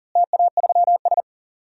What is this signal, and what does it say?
Morse code, heard as a keyed tone, about 40 wpm, 700 Hz. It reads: TA3S
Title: DXCC-100 Call Signs - 4 Characters - Repeated 3x - 40wpm